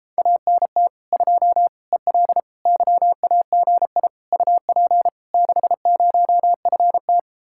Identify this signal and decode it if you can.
Morse code, heard as a keyed tone, about 31 wpm, 700 Hz. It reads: ANT 2 EL YAGI UP 60FT